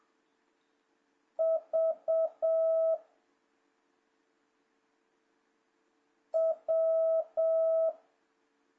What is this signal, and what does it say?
Morse code, heard as a keyed tone, about 7 wpm, 650 Hz. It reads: V W
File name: jh8cbh / 2026cw